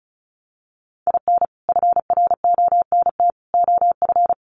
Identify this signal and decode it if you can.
Morse code, heard as a keyed tone, about 34 wpm, 700 Hz. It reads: IN FRONT OF
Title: Sets of 3 Words 35wpm